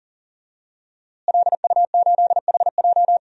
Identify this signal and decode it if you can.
Morse code, heard as a keyed tone, about 40 wpm, 700 Hz. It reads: LU8HJ
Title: DXCC-100 Call Signs - 5 Characters - Repeated 3x - 40wpm